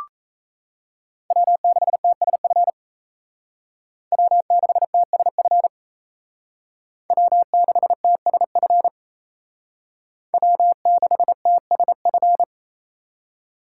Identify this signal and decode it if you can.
Morse code, heard as a keyed tone, about 36 wpm, 700 Hz. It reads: W6TSF W6TSF W6TSF W6TSF